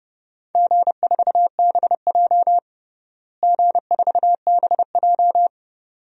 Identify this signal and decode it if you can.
Morse code, heard as a keyed tone, about 30 wpm, 700 Hz. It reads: G4BJ G4BJ
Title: DXCC-100 Call Signs - 4 Characters - Repeated 2x - 30wpm